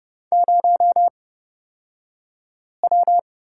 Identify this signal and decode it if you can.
Morse code, heard as a keyed tone, about 30 wpm, spaced as for 5 wpm, 700 Hz. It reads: 0W
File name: Lesson 38d - Letter q - Two Characters 30wpm